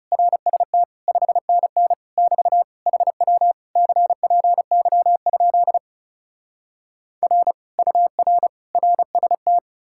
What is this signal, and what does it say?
Morse code, heard as a keyed tone, about 37 wpm, 700 Hz. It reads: RST 5NN <BT> HW CPY? R UR RST
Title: Speed-Racing Common QSO Phrases Encore - 30wpm